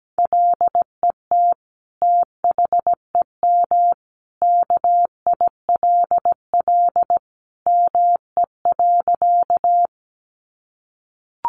Morse code, heard as a keyed tone, about 17 wpm, 700 Hz. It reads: LET THEM KILL ME.
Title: Sentences from Top 1400 Words 17wpm